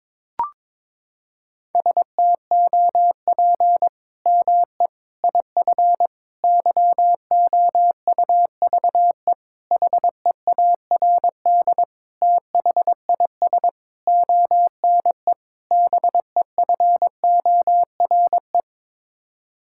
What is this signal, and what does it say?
Morse code, heard as a keyed tone, about 22 wpm, 700 Hz. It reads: STOP ME IF YOUVE HEARD THIS ONE BEFORE
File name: practice / 22wpm / Sets of 8 Words - 22wpm